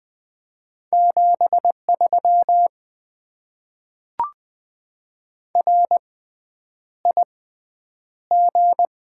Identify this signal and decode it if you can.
Morse code, heard as a keyed tone, about 20 wpm, spaced as for 9 wpm, 700 Hz. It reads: 73 RIG
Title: Top 90 QSO Elements - 6x Character Spacing - 20wpm